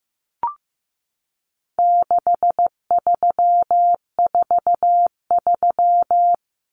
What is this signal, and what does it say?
Morse code, heard as a keyed tone, about 15 wpm, 700 Hz. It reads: 6343
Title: Common Cut Numbers - 4-Digit 15wpm